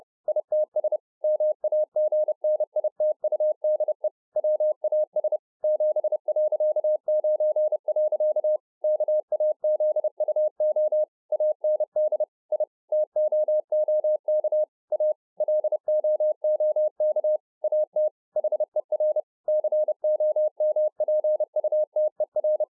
Morse code, heard as a keyed tone, about 30 wpm, 610 Hz. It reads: ITS MAGNITUDE WAS 7.9. KAZUO AND I TOOK A LOOK AT HER COMPUTER